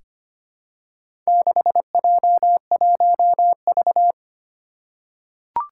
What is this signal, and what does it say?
Morse code, heard as a keyed tone, about 25 wpm, 700 Hz. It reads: 6J1V
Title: Four Challenging Letter-Number Combinations 25wpm